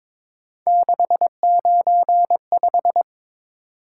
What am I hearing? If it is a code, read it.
Morse code, heard as a keyed tone, about 22 wpm, 700 Hz. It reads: 695
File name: practice / 22wpm / Three Letter-Number Combinations - 22wpm